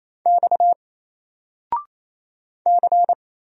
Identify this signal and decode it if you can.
Morse code, heard as a keyed tone, about 27 wpm, 700 Hz. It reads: X C
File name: Single-Letter-Number-28wpm